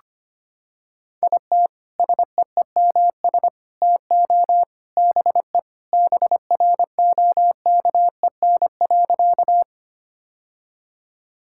Morse code, heard as a keyed tone, about 25 wpm, 700 Hz. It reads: IT SEEMS TO BE BROKEN.
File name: practice / 25wpm / Sentences with 3-Word Phrases - 25wpm